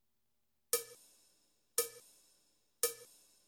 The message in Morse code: none